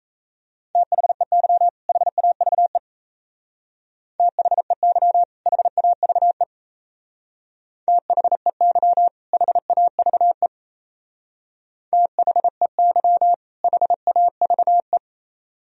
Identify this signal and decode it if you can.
Morse code, heard as a keyed tone, about 34 wpm, 700 Hz. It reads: THEY HAVE THEY HAVE THEY HAVE THEY HAVE
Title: Speed-Racing - Sets of 2 Words 28wpm